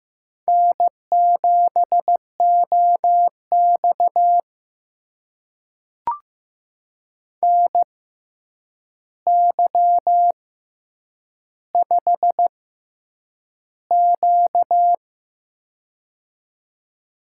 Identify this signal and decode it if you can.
Morse code, heard as a keyed tone, about 15 wpm, spaced as for 8 wpm, 700 Hz. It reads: N7OX NY5Q
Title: USA Call Signs - 4 Characters - 6x Character Spacing - 15wpm